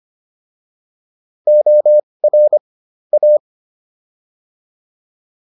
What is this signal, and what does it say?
Morse code, heard as a keyed tone, about 25 wpm, spaced as for 20 wpm, 600 Hz. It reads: OR A